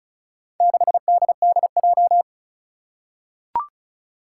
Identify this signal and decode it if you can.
Morse code, heard as a keyed tone, about 36 wpm, 700 Hz. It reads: 6DDJ
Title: Four Challenging Letter-Number Combinations 35wpm